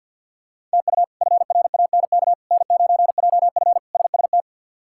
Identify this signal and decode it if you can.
Morse code, heard as a keyed tone, about 51 wpm, 700 Hz. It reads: TU FRANK N9JF SST
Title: K1USN SST QSO Practice - 50wpm